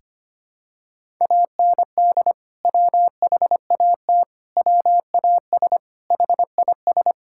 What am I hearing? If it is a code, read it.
Morse code, heard as a keyed tone, about 25 wpm, 700 Hz. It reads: AND WHAT WAS HIS
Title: Sentences from Top 1600 Words 25wpm